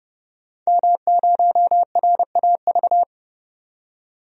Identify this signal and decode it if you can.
Morse code, heard as a keyed tone, about 30 wpm, 700 Hz. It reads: M0RAV
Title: DXCC-100 Call Signs - 5 Characters - 30wpm